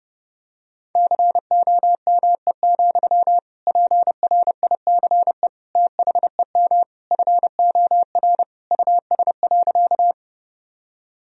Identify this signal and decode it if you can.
Morse code, heard as a keyed tone, about 30 wpm, 700 Hz. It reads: COME, PRICE THEM FOR US.
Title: Sentences from Top 1000 Words - Review Part 4 - 30wpm